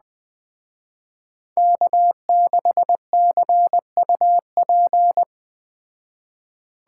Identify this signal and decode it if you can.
Morse code, heard as a keyed tone, about 20 wpm, 700 Hz. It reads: K6CUP
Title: Speed Racing - USA 1x3 Call Signs 17wpm